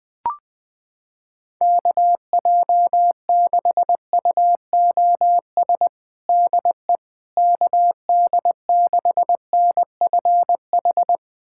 Morse code, heard as a keyed tone, about 20 wpm, 700 Hz. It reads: KJ6UOS DE KD6NFH
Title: USA Callsign DE Callsign - 6 Characters - 20wpm